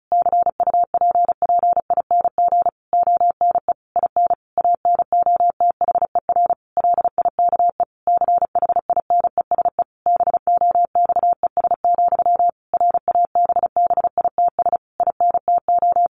Morse code, heard as a keyed tone, about 34 wpm, 700 Hz. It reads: CUPPING ONE IN ANOTHER LIKE CHINESE BOXES, RABBITS INTO
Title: A Sound Of Thunder 35wpm